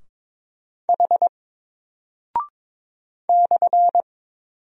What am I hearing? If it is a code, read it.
Morse code, heard as a keyed tone, about 22 wpm, 700 Hz. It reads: H /
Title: Single Character (Letter:Number:Punctuation:Prosign) Alt 22wpm